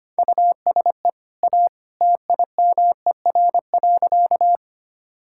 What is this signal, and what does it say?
Morse code, heard as a keyed tone, about 25 wpm, 700 Hz. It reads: USE A TIMER.
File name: Easy Sentences with Hints - 25wpm